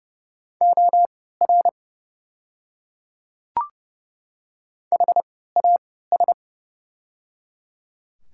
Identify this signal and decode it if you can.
Morse code, heard as a keyed tone, about 30 wpm, spaced as for 17 wpm, 690 Hz. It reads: OR HAS
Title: 3x Character Spacing - Top 200 Words - 30wpm